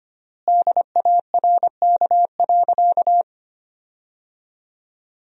Morse code, heard as a keyed tone, about 25 wpm, 700 Hz. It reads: DARK.
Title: 4x Word-Spacing - Sentences from Top 500 Words - 25wpm